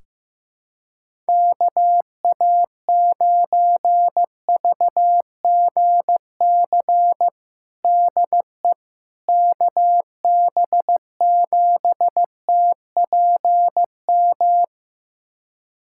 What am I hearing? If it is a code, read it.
Morse code, heard as a keyed tone, about 15 wpm, 700 Hz. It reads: KA9VGC DE KB7TPM
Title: USA Callsign DE Callsign - 6 Characters - 15wpm